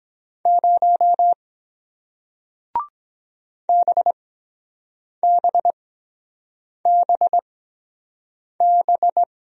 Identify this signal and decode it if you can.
Morse code, heard as a keyed tone, about 23 wpm, spaced as for 7 wpm, 700 Hz. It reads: 0 BBBB